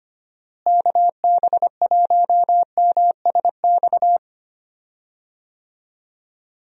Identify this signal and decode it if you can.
Morse code, heard as a keyed tone, about 25 wpm, 700 Hz. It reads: KB1MSX
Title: USA Call Signs - 6 Characters - Repeated 3x - 25wpm